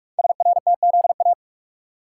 Morse code, heard as a keyed tone, about 44 wpm, 700 Hz. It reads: SRTZA
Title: Code Groups - 5 Letters - 45wpm